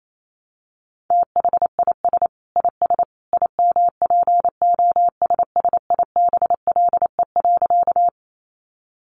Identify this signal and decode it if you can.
Morse code, heard as a keyed tone, about 28 wpm, 700 Hz. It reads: THIS IS IMPOSSIBLE.
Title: A Sound Of Thunder 28wpm